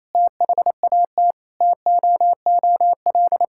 Morse code, heard as a keyed tone, about 28 wpm, 700 Hz. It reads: THAT TOOL
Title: Sentences from Top 1000 Words - Review Part 6 - 28wpm